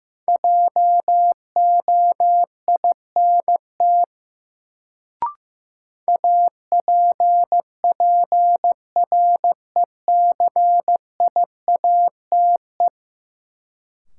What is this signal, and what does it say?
Morse code, heard as a keyed tone, about 15 wpm, 700 Hz. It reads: JOINT APPRECIATE
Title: Sentences from Top 1300 Words - New Word Drill Alt - 15wpm